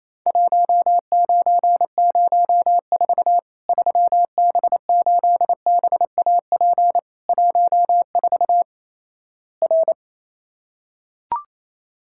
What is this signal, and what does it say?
Morse code, heard as a keyed tone, about 28 wpm, 700 Hz. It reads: 1904 3B8BAP 14 R
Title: Worked All Europe DX Contest - QTC Exchange - 28wpm